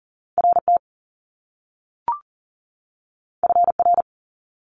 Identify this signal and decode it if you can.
Morse code, heard as a keyed tone, about 42 wpm, 700 Hz. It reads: RT FR
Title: Top Two Letter Combinations in Words 40wpm